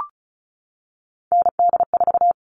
Morse code, heard as a keyed tone, about 34 wpm, 700 Hz. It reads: ND4
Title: All Cut Numbers - 3-Digit 35wpm